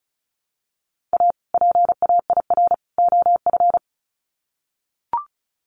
Morse code, heard as a keyed tone, about 34 wpm, 700 Hz. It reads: A PAIR OF
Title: Sets of 3 Words 35wpm